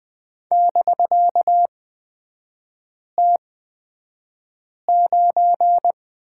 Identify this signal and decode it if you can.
Morse code, heard as a keyed tone, about 20 wpm, spaced as for 5 wpm, 700 Hz. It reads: <BK>T9
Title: 250 Lesson 40e - BK - Three Characters 20wpm